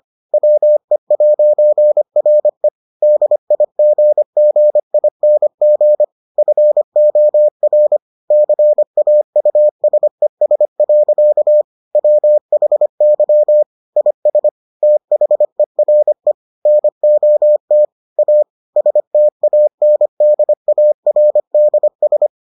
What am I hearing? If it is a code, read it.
Morse code, heard as a keyed tone, about 25 wpm, 600 Hz. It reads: WE'RE DIGGING FOR CAUSES. WHY IS THERE NOT A STANDARDS